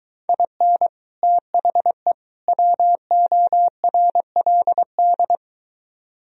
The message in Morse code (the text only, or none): IN THE WORLD